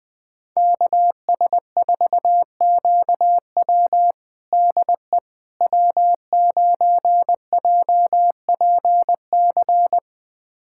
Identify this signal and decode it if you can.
Morse code, heard as a keyed tone, about 21 wpm, 700 Hz. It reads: KS4QW DE W9JPC
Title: USA Callsign DE Callsign - 5 Characters - 20wpm